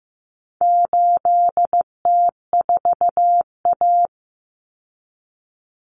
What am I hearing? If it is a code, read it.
Morse code, heard as a keyed tone, about 15 wpm, 700 Hz. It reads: 8T4A
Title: Common Cut Numbers - 4-Digit 15wpm